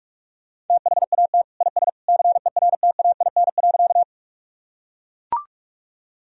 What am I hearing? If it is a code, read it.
Morse code, heard as a keyed tone, about 41 wpm, 700 Hz. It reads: THAT IS CERTAIN.
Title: Sentences from Top 1000 Words - Review Part 2 - 45wpm